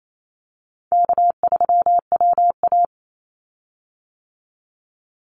Morse code, heard as a keyed tone, about 28 wpm, 700 Hz. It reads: K3WA
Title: Easy Call Signs - 28wpm